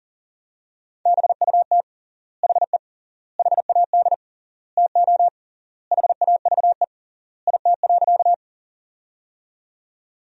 Morse code, heard as a keyed tone, about 40 wpm, 700 Hz. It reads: BUT HE HAD TO HAVE IT.
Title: 3x Word-Spacing - Sentences from Top 100 Words - 40wpm